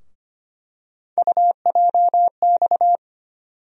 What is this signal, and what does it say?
Morse code, heard as a keyed tone, about 25 wpm, 700 Hz. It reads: UJX
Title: USA Call Sign Postfixes - 3 Letters 25wpm